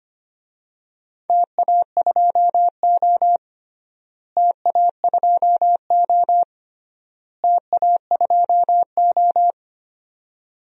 Morse code, heard as a keyed tone, about 25 wpm, 700 Hz. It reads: TA2O TA2O TA2O
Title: DXCC-100 Call Signs - 4 Characters - Repeated 3x - 25wpm